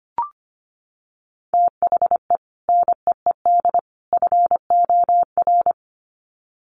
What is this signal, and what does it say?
Morse code, heard as a keyed tone, about 25 wpm, 700 Hz. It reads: THE NEED FOR
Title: Sets of 3 Words 25wpm